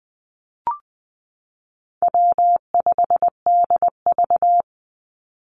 Morse code, heard as a keyed tone, about 20 wpm, 700 Hz. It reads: W5DV
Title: Call Signs Repeated 3x - 20wpm